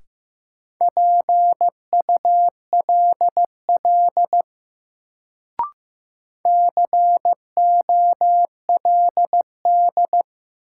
Morse code, heard as a keyed tone, about 15 wpm, 700 Hz. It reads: PULL COLD
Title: Sentences from Top 400 Words - New Word Drill - 15wpm